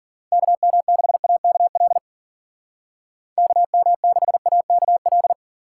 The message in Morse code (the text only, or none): KM6AKL KM6AKL